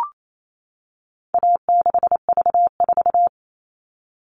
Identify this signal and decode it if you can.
Morse code, heard as a keyed tone, about 28 wpm, 700 Hz. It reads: A6V4
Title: All Cut Numbers - 4-Digit 28wpm